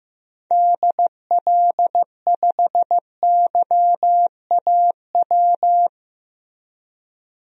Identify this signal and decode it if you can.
Morse code, heard as a keyed tone, about 15 wpm, 700 Hz. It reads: DL5YAW
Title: DXCC-100 Call Signs - 6 Characters - Repeated 3x - 15wpm